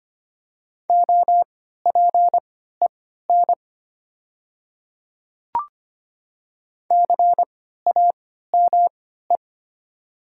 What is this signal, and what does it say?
Morse code, heard as a keyed tone, about 25 wpm, spaced as for 14 wpm, 700 Hz. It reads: OPEN CAME
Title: 3x Character Spacing - Top 300 Words - 25wpm